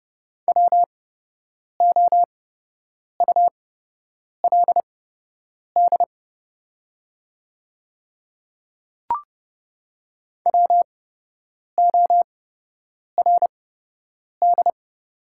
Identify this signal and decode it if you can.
Morse code, heard as a keyed tone, about 30 wpm, spaced as for 8 wpm, 700 Hz. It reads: WOULD WORD
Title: 8x Character Spacing - Top 100 Words - 30wpm